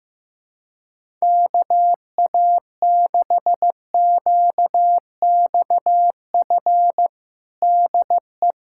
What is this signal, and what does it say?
Morse code, heard as a keyed tone, about 15 wpm, 700 Hz. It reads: KA6QXF DE
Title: USA Callsign DE Callsign - 6 Characters - 15wpm